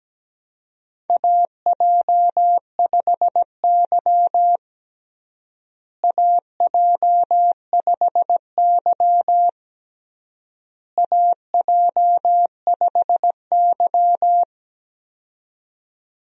Morse code, heard as a keyed tone, about 17 wpm, 700 Hz. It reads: AJ5Y AJ5Y AJ5Y
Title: USA Call Signs - 4 Characters - Repeated 3x - 17wpm